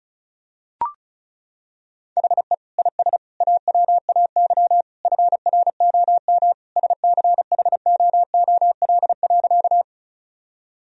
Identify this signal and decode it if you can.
Morse code, heard as a keyed tone, about 34 wpm, 700 Hz. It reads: HE IS AWAY FROM SCHOOL.
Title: Sentences from Top 1000 Words - Review Part 1 - 35wpm